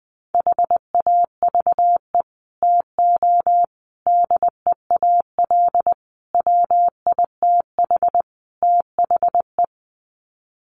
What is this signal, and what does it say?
Morse code, heard as a keyed tone, about 20 wpm, 700 Hz. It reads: HAVE TO DEAL WITH THE
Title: Sets of 5 Words Encore2 20wpm